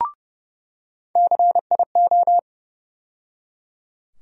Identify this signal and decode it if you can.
Morse code, heard as a keyed tone, about 30 wpm, 700 Hz. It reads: CIO
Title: USA Call Sign Postfixes - 3 Letters 30wpm